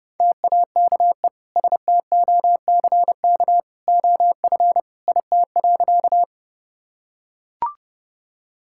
Morse code, heard as a keyed tone, about 30 wpm, 700 Hz. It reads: TAKE STOCK OF IT.
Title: Sentences from Top 1000 Words - Review Part 5 - 30wpm